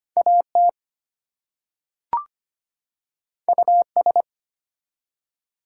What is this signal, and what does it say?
Morse code, heard as a keyed tone, about 25 wpm, 700 Hz. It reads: AT US
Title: Two Letter Words - 25wpm